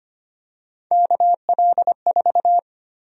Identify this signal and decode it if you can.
Morse code, heard as a keyed tone, about 25 wpm, 700 Hz. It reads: KL4
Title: USA Call Sign Prefixes - 25wpm